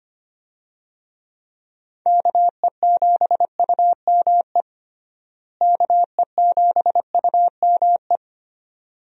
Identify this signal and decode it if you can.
Morse code, heard as a keyed tone, about 25 wpm, 700 Hz. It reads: KE7UME KE7UME
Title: USA Call Signs - 6 Characters - Repeated 2x - 25wpm